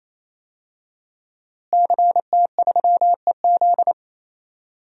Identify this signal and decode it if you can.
Morse code, heard as a keyed tone, about 27 wpm, 700 Hz. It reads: CT3EZ
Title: DXCC-100 Call Signs - 5 Characters - Repeated 3x - 28wpm